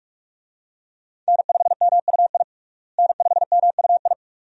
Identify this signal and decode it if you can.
Morse code, heard as a keyed tone, about 43 wpm, 700 Hz. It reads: N5MUI N5MUI
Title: USA Call Signs - 5 Characters - Repeated 2x - 45wpm